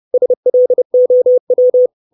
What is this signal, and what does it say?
Morse code, heard as a keyed tone, about 30 wpm, 500 Hz. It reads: SLOW
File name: 632 Words 30WPM